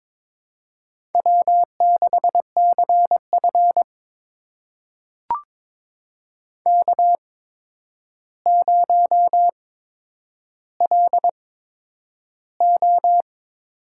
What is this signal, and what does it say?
Morse code, heard as a keyed tone, about 22 wpm, spaced as for 9 wpm, 700 Hz. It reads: W6CF K0LO
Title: USA Call Signs - 4 Characters - 8x Character Spacing - 22wpm